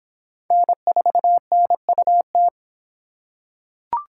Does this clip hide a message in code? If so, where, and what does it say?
Yes: Morse code from 0.5 to 2.5 s, heard as a keyed tone, about 23 wpm, 700 Hz. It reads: N4NUT